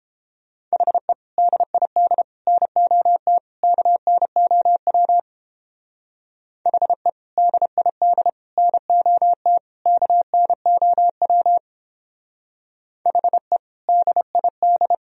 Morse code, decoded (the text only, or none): HE DID NOT KNOW HE DID NOT KNOW HE DID